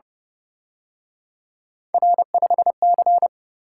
Morse code, heard as a keyed tone, about 31 wpm, 700 Hz. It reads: R5C
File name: Speed-Racing - Three Letter-Number Combinations - 25wpm